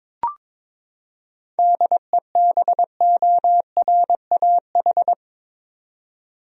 Morse code, heard as a keyed tone, about 22 wpm, 700 Hz. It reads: DEBORAH
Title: USA Common Names - 22wpm